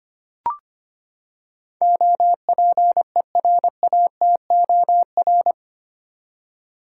Morse code, heard as a keyed tone, about 25 wpm, 700 Hz. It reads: OPERATOR
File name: Eight Letter Words - 25wpm